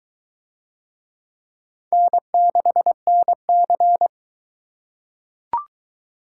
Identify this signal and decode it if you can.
Morse code, heard as a keyed tone, about 23 wpm, 700 Hz. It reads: N6NC